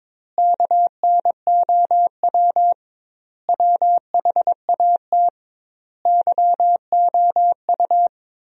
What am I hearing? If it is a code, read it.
Morse code, heard as a keyed tone, about 22 wpm, 700 Hz. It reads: KNOW WHAT YOU
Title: 2x Word-Spacing - Sentences from Top 100 Words - 22wpm